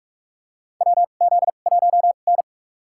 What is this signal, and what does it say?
Morse code, heard as a keyed tone, about 48 wpm, spaced as for 33 wpm, 700 Hz. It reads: WZ1N
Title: USA Call Signs - 4 Characters - 2x Character Spacing - 45wpm